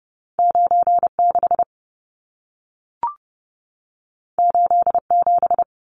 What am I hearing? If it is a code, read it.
Morse code, heard as a keyed tone, about 30 wpm, 700 Hz. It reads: 96 87
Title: Numbers - 2-Digit 30wpm